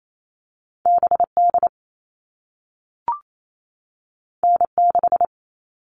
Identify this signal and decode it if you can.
Morse code, heard as a keyed tone, about 28 wpm, 700 Hz. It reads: BD N6
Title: All Cut Numbers - 2-Digit 28wpm